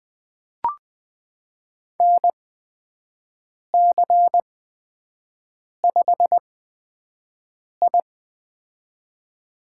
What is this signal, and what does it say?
Morse code, heard as a keyed tone, about 20 wpm, spaced as for 5 wpm, 700 Hz. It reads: NC5I
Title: USA Call Signs - 4 Characters - 8x Character Spacing - 20wpm